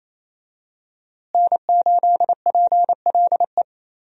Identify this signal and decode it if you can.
Morse code, heard as a keyed tone, about 28 wpm, 700 Hz. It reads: N8PLE
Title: USA Call Signs - 5 Characters - Repeated 3x - 28wpm